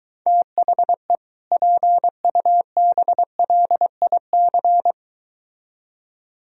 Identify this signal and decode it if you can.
Morse code, heard as a keyed tone, about 23 wpm, 700 Hz. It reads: THE PUBLIC